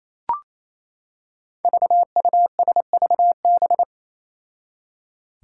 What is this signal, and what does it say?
Morse code, heard as a keyed tone, about 28 wpm, 700 Hz. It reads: VUSVB
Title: Five Challenging Letter-Number Combinations 28wpm